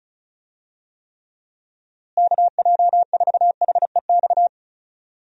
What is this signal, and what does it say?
Morse code, heard as a keyed tone, about 36 wpm, 700 Hz. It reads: KJ4HEX
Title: USA Call Signs - 6 Characters - Repeated 2x - 35wpm